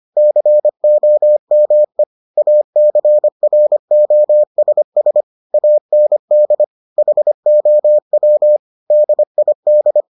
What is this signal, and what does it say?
Morse code, heard as a keyed tone, about 25 wpm, 600 Hz. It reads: COME ACROSS AND HOW DID